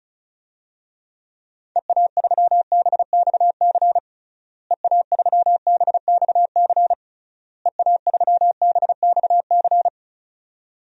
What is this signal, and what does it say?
Morse code, heard as a keyed tone, about 36 wpm, 700 Hz. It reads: EA3BXC EA3BXC EA3BXC
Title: DXCC-100 Call Signs - 6 Characters - Repeated 3x - 35wpm